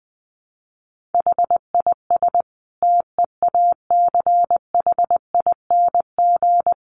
Morse code, heard as a keyed tone, about 20 wpm, 700 Hz. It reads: HIS TEACHING
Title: Sentences from Top 1100 Words 20wpm